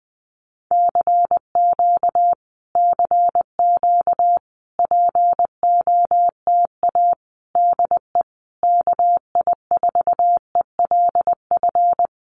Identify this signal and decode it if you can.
Morse code, heard as a keyed tone, about 20 wpm, 700 Hz. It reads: CQ CQ POTA DE KI4ELF
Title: Common QSO Phrases Encore 20wpm